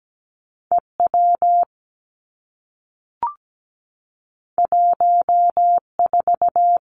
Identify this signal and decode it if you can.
Morse code, heard as a keyed tone, about 17 wpm, 700 Hz. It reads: E W 1 4
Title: Two Challenging Letter-Number Combinations 17wpm